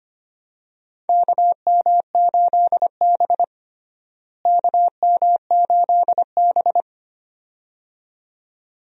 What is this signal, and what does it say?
Morse code, heard as a keyed tone, about 25 wpm, 700 Hz. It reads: KM8B KM8B
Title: USA Call Signs - 4 Characters - Repeated 2x - 25wpm